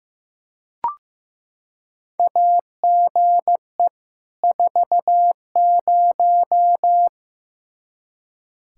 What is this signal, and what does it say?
Morse code, heard as a keyed tone, about 16 wpm, 700 Hz. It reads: AGE 40
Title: Practice QSO - Exchange 3 - More Info - 15wpm